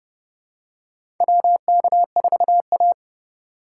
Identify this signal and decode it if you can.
Morse code, heard as a keyed tone, about 30 wpm, 700 Hz. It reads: WK4A
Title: USA Call Signs - 4 Characters - 6x Character Spacing - 30wpm